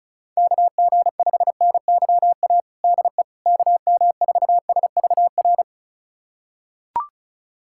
Morse code, heard as a keyed tone, about 36 wpm, 700 Hz. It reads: KG5NYA DE KM4SVR
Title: USA Callsign DE Callsign - 6 Characters - 35wpm